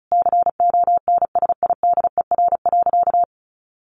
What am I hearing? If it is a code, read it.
Morse code, heard as a keyed tone, about 34 wpm, 700 Hz. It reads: CONSIDER.
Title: Sentences from Top 900 Words 35wpm